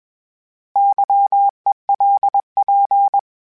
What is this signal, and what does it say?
Morse code, heard as a keyed tone, about 21 wpm, 800 Hz. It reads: YELP
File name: Misc / Misc-mp3s / Misc @20